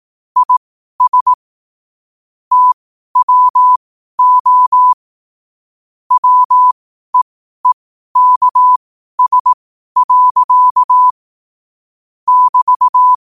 Morse code, heard as a keyed tone, about 18 wpm, spaced as for 12 wpm, 1000 Hz. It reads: IS TWO WEEKS. <BT>
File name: qotdfiles / 12wpm